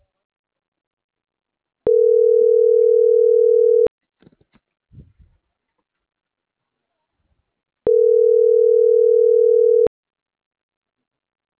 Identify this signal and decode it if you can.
Ringback tone (North America), 440 + 480 Hz.